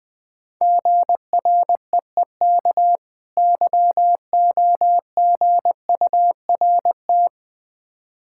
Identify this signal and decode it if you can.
Morse code, heard as a keyed tone, about 21 wpm, 690 Hz. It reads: GREEK YOGURT
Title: Difficult Words with Hints - 20wpm